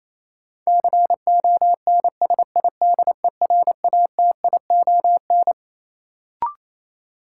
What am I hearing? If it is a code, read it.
Morse code, heard as a keyed tone, about 28 wpm, 700 Hz. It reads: CONSIDERATION
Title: Sentences from Top 1500 Words - New Word Drill Alt - 28wpm